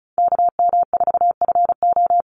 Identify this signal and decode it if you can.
Morse code, heard as a keyed tone, about 34 wpm, 700 Hz. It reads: KM4FO
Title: Call Signs Repeated 3x - 35wpm